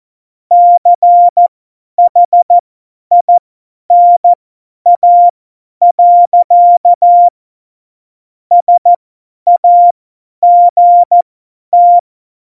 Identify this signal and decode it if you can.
Morse code, heard as a keyed tone, about 14 wpm, spaced as for 10 wpm, 700 Hz. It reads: CHINA. SAGT
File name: Morsekurs / Witze / Witz 07 April 2020, 50 F70